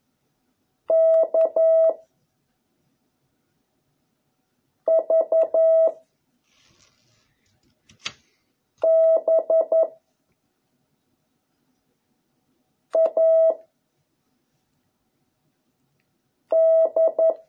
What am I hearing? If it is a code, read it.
Morse code, heard as a keyed tone, about 11 wpm, 660 Hz. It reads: K V B A D